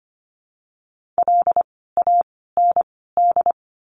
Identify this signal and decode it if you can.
Morse code, heard as a keyed tone, about 25 wpm, 700 Hz. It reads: L A N D
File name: course / 25:10wpm / 094 Lesson 16f - Letter c - Words 25wpm